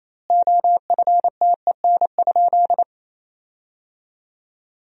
Morse code, heard as a keyed tone, about 28 wpm, 700 Hz. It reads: OFTEN?
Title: 2x Word-Spacing - Sentences from Top 400 Words - 28wpm